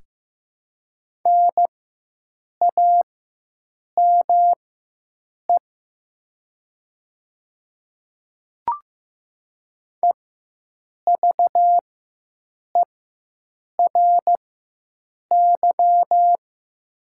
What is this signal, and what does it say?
Morse code, heard as a keyed tone, about 15 wpm, spaced as for 7 wpm, 700 Hz. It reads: NAME EVERY
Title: 4x Character Spacing - Top 300 Words - 15wpm